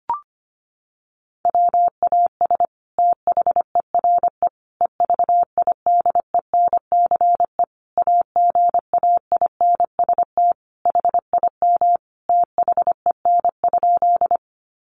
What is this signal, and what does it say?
Morse code, heard as a keyed tone, about 25 wpm, 700 Hz. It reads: WAS THERE EVIDENCE AGAINST HIM THEN?
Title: Sentences from Top 1300 Words 25wpm